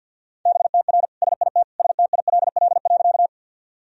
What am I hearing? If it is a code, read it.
Morse code, heard as a keyed tone, about 52 wpm, 700 Hz. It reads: BTR SIT STILL.
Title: Sentences from Top 500 Words and Abbreviations 50wpm